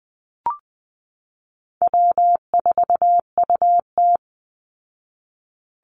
Morse code, heard as a keyed tone, about 19 wpm, 700 Hz. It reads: W4UT
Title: Easy Call Signs Repeated 3x - 20wpm